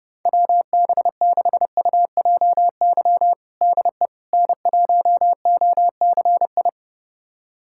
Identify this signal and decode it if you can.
Morse code, heard as a keyed tone, about 32 wpm, 700 Hz. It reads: WB6UJY DE N1OCI
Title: USA Callsign DE Callsign 30wpm